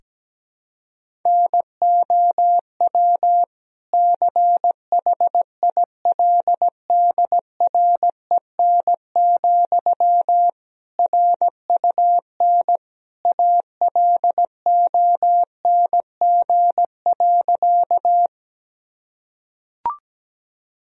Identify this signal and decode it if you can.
Morse code, heard as a keyed tone, about 17 wpm, 700 Hz. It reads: NOW CHILDREN, RUN ALONG.
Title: Sentences from Top 1000 Words - Review Part 1 - 17wpm